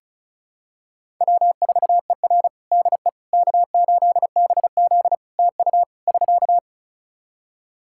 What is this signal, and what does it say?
Morse code, heard as a keyed tone, about 36 wpm, 700 Hz. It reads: W4ER DE K8BZ TU <SK>
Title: Practice QSO - Exchange 4 - Ending - 35wpm